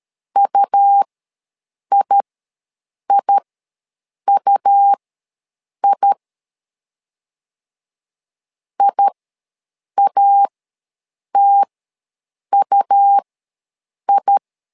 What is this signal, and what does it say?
Morse code, heard as a keyed tone, about 13 wpm, spaced as for 7 wpm, 800 Hz. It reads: UIIUI IATUI
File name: LEZIONI / 2 UHDN / 2 U ws 12wpm 1655